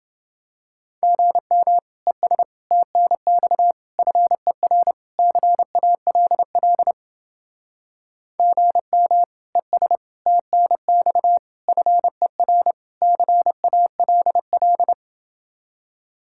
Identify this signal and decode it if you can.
Morse code, heard as a keyed tone, about 30 wpm, 700 Hz. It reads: GM ES TNX FER CALL GM ES TNX FER CALL